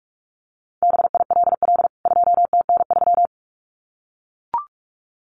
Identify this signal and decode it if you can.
Morse code, heard as a keyed tone, about 44 wpm, 700 Hz. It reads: BILL 2TN3
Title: CWT Exchange 45wpm